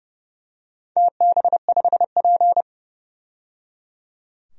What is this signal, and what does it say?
Morse code, heard as a keyed tone, about 29 wpm, 700 Hz. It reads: TB5P
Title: DXCC-100 Call Signs - 4 Characters - 30wpm